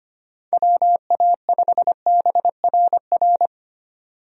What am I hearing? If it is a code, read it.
Morse code, heard as a keyed tone, about 25 wpm, 700 Hz. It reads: WA5BRR
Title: USA Call Signs - 6 Characters - 25wpm